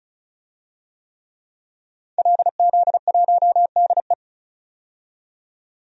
Morse code, heard as a keyed tone, about 36 wpm, 700 Hz. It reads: LZ1DE
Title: DXCC-100 Call Signs - 5 Characters - 35wpm